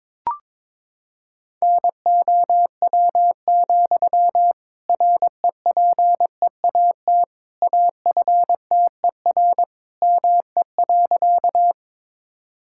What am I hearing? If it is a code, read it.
Morse code, heard as a keyed tone, about 22 wpm, 700 Hz. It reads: NOW, REPEAT AFTER ME.
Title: Sentences from Top 1600 Words 22wpm